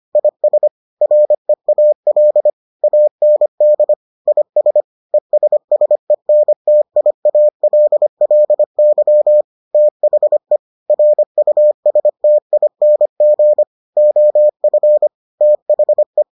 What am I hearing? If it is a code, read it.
Morse code, heard as a keyed tone, about 25 wpm, 600 Hz. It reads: IS REAL AND IS ESSENTIALLY THE RUSTING OF THE